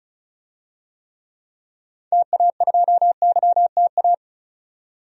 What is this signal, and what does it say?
Morse code, heard as a keyed tone, about 36 wpm, 700 Hz. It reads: TA2YTA
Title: DXCC-100 Call Signs - 6 Characters - 35wpm